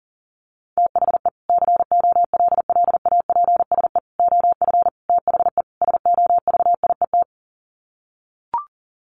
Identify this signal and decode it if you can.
Morse code, heard as a keyed tone, about 37 wpm, 700 Hz. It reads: THE COLLAPSE OF THE SOVIET
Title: Sets of 5 Words Encore 40wpm